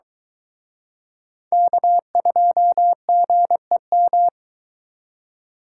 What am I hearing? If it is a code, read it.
Morse code, heard as a keyed tone, about 23 wpm, 700 Hz. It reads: K2GEM